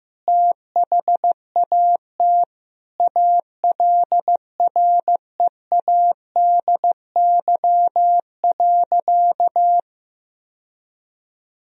Morse code, heard as a keyed tone, about 15 wpm, 700 Hz. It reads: THAT ALREADY.
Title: Sentences from Top 1000 Words - Review Part 3 - 15wpm